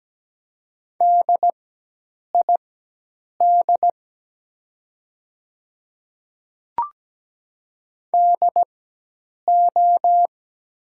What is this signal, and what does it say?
Morse code, heard as a keyed tone, about 17 wpm, spaced as for 8 wpm, 700 Hz. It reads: DID DO
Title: 4x Character Spacing - Top 100 Words - 17wpm